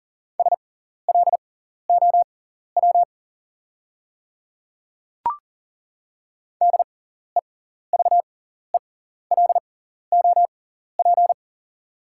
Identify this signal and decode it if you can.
Morse code, heard as a keyed tone, about 40 wpm, spaced as for 14 wpm, 690 Hz. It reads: SLOW DEVELOP
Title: 6x Character Spacing - Top 500 Words - 40wpm